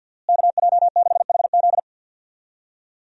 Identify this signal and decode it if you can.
Morse code, heard as a keyed tone, about 50 wpm, 700 Hz. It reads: KJ6HZ
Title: USA Call Signs - 5 Characters - Repeated 2x - 50wpm